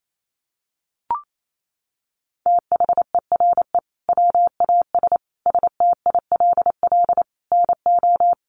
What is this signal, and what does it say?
Morse code, heard as a keyed tone, about 28 wpm, 700 Hz. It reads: THERE WAS STILL NO
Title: Sentences from Top 1400 Words 28wpm